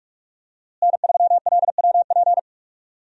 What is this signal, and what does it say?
Morse code, heard as a keyed tone, about 47 wpm, 700 Hz. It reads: N3LWP
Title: USA Call Signs - 5 Characters - Repeated 2x - 45wpm